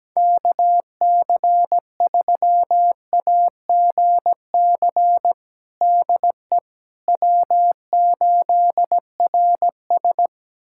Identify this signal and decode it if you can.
Morse code, heard as a keyed tone, about 17 wpm, 700 Hz. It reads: KC3AGC DE W8RS
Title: USA Callsign DE Callsign 17wpm